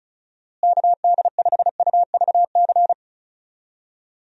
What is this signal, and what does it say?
Morse code, heard as a keyed tone, about 36 wpm, 700 Hz. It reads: KD5UVC
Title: USA Call Signs - 6 Characters - Repeated 3x - 35wpm